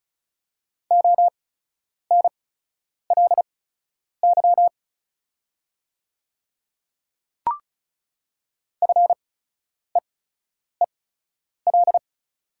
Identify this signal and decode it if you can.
Morse code, heard as a keyed tone, about 34 wpm, spaced as for 10 wpm, 700 Hz. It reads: ONLY FEEL
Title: 8x Character Spacing - Top 500 Words - 35wpm